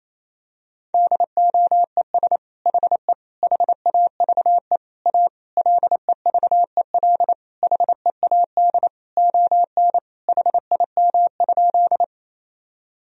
Morse code, heard as a keyed tone, about 28 wpm, 700 Hz. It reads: DOES HE HAVE A LEVEL HEAD ON HIM?
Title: Sentences from Top 1000 Words - Review Part 4 - 28wpm